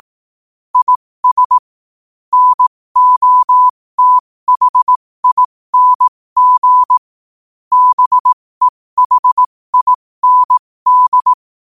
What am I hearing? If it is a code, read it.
Morse code, heard as a keyed tone, about 18 wpm, spaced as for 15 wpm, 1000 Hz. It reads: IS NOTHING BEHIND
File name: qotdfiles / 15wpm